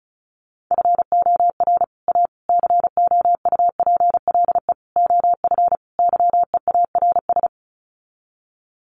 Morse code, heard as a keyed tone, about 34 wpm, 700 Hz. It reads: FOR A COUPLE OF YEARS
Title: Sets of 5 Words Encore 35wpm